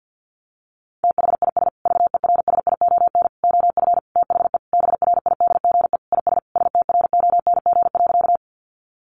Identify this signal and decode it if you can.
Morse code, heard as a keyed tone, about 48 wpm, 700 Hz. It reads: THIS VERSION OF THE BRIDGE IS STRONG.
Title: Sentences from Top 1400 Words 50wpm